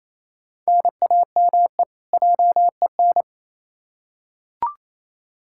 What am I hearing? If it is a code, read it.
Morse code, heard as a keyed tone, about 28 wpm, 700 Hz. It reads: NAME JEN
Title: K1USN SST QSO Practice 28wpm